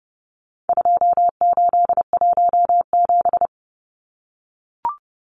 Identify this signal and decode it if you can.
Morse code, heard as a keyed tone, about 30 wpm, 700 Hz. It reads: 2817
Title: Numbers - 4-Digit 30wpm